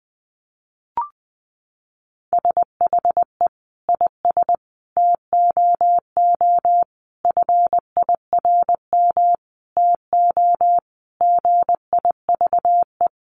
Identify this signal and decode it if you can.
Morse code, heard as a keyed tone, about 20 wpm, 700 Hz. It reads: SHE IS TOO FIRM TO GIVE
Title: Sentences from Top 1400 Words 20wpm